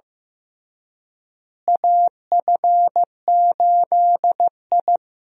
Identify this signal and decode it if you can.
Morse code, heard as a keyed tone, about 15 wpm, 700 Hz. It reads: AF8I
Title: USA Call Signs - 4 Characters - Repeated 3x - 15wpm